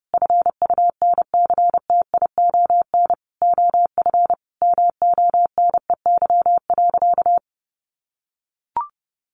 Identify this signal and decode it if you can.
Morse code, heard as a keyed tone, about 30 wpm, 700 Hz. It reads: FUNCTION OF MONEY.
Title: Sentences from Top 1100 Words 30wpm